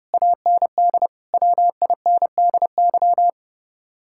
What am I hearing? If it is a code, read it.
Morse code, heard as a keyed tone, about 37 wpm, 700 Hz. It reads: AND WINDY
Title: Speed-Racing Common QSO Phrases Encore - 30wpm